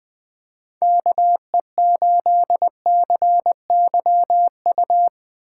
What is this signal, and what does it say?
Morse code, heard as a keyed tone, about 22 wpm, 700 Hz. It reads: KE8CYU
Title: Speed Racing - USA Call Signs 20wpm